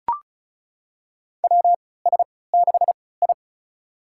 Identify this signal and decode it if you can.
Morse code, heard as a keyed tone, about 34 wpm, spaced as for 20 wpm, 700 Hz. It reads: WS6I
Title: USA Call Signs - 4 Characters - 3x Character Spacing - 35wpm